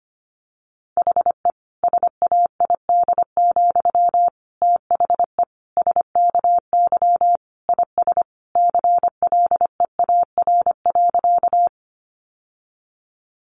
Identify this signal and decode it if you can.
Morse code, heard as a keyed tone, about 25 wpm, 700 Hz. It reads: HE SAID, THE SKY IS CLEAR.
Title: Sentences from Top 1200 Words 25wpm